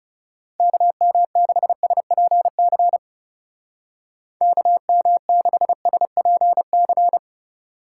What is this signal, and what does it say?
Morse code, heard as a keyed tone, about 37 wpm, 700 Hz. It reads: KM6SPC KM6SPC
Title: Speed Racing - USA Call Signs 30wpm